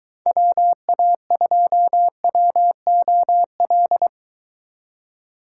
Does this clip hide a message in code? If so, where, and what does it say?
Yes: Morse code from 0.3 to 4.1 s, heard as a keyed tone, about 22 wpm, 700 Hz. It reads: WA2WOL